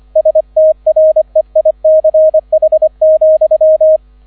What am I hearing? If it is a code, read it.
Morse code, heard as a keyed tone, about 25 wpm, 630 Hz, with radio noise. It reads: STREICH,